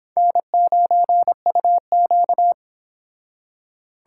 Morse code, heard as a keyed tone, about 25 wpm, 700 Hz. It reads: N9UQ